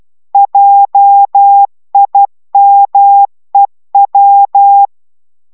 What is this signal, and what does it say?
Morse code, heard as a keyed tone, about 12 wpm, 800 Hz. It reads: JIMEW